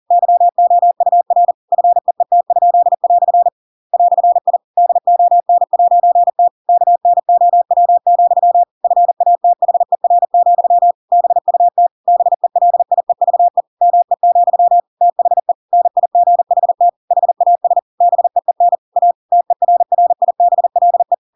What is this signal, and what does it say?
Morse code, heard as a keyed tone, about 40 wpm, 700 Hz. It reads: YOUR FEET?" "I DON'T KNOW, FATHER, BUT BELIEVE ME, THE NIGHT HAS BEEN A TERRIBLE
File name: Pinocchio / Pinocchio ch7 40 40 ik1hge